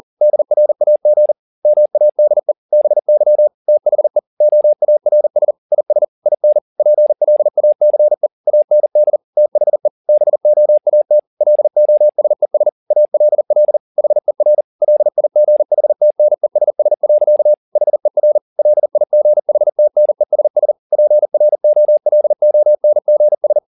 Morse code, heard as a keyed tone, about 40 wpm, 600 Hz. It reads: DRAG MADE BY THE OARS IS IN PLACE AND THE BOAT LOSES ALL HER LIGHTNESS. HER LIGHTNESS PROLONGS
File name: the old man and the sea 2 40wpm